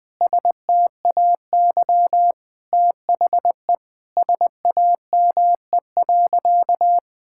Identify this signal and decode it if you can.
Morse code, heard as a keyed tone, about 20 wpm, 700 Hz. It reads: STAY THE SAME.
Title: Sentences with 7-Word Phrases - Spoken Beginning - 20wpm